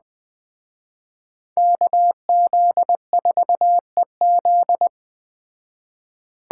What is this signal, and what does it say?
Morse code, heard as a keyed tone, about 20 wpm, 700 Hz. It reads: KZ4EZ
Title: Speed Racing - USA Call Signs 17wpm